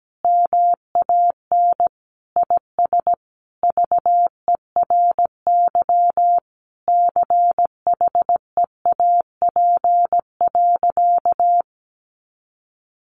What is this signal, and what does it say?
Morse code, heard as a keyed tone, about 17 wpm, 700 Hz. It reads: MAN IS VERY CHEAP.
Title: Sentences from Top 1200 Words 17wpm